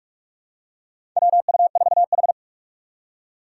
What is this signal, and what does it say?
Morse code, heard as a keyed tone, about 46 wpm, 700 Hz. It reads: WU4H
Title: USA Call Signs - 4 Characters - 2x Character Spacing - 45wpm